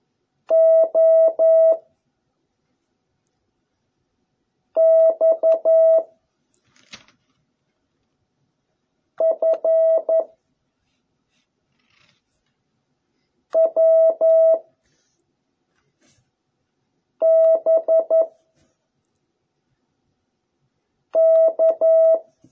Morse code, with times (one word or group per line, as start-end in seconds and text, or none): 0.5-1.7 O
4.8-6.0 X
9.2-10.2 F
13.6-14.6 W
17.2-18.2 B
21.2-22.2 K